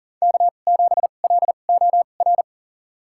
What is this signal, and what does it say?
Morse code, heard as a keyed tone, about 40 wpm, spaced as for 29 wpm, 700 Hz. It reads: K7LOR
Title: USA Call Signs - 2x Character Spacing - 40wpm